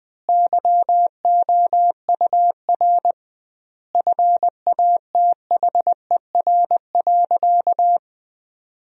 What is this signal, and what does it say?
Morse code, heard as a keyed tone, about 20 wpm, 700 Hz. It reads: YOUR FATHER.
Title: 2x Word-Spacing - Sentences from Top 300 Words - 20wpm